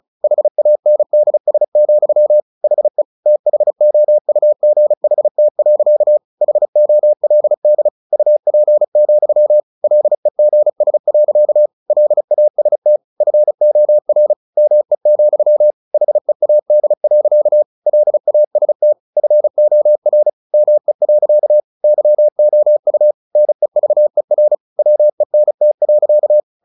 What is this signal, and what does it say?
Morse code, heard as a keyed tone, about 34 wpm, 600 Hz. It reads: HANDS, HE THOUGHT. HOLD UP, LEGS. LAST FOR ME, HEAD. LAST FOR ME. YOU NEVER WENT.
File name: the old man and the sea 2 35wpm